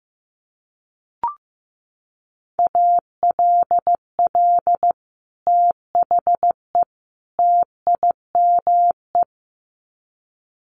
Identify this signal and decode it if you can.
Morse code, heard as a keyed tone, about 15 wpm, 700 Hz. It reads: ALL THE TIME
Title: Sets of 3 Words 15wpm